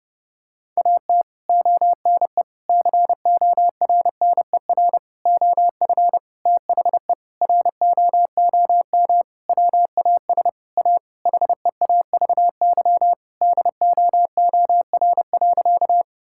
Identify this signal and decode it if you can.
Morse code, heard as a keyed tone, about 32 wpm, 700 Hz. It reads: AT ONE CORNER OF THE ROOM WAS A HEAVY DOOR.
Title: Sentences from Top 1600 Words 30wpm